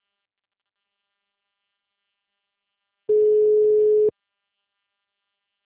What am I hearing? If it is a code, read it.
Ringback tone (Europe, 425 Hz).
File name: out- -2013- - - .46946